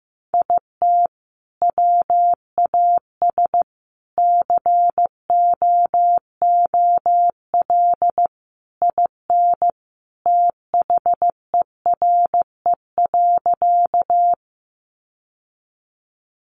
Morse code, heard as a keyed tone, about 15 wpm, 700 Hz. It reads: IT WAS COOL IN THERE.
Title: Sentences from Top 1200 Words 15wpm